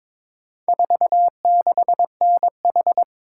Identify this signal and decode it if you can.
Morse code, heard as a keyed tone, about 22 wpm, 700 Hz. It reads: 46NH
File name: practice / 22wpm / Four Letter-Number Combinations - 22wpm